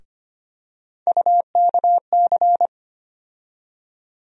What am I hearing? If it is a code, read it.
Morse code, heard as a keyed tone, about 25 wpm, 700 Hz. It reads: UKC
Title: USA Call Sign Postfixes - 3 Letters 25wpm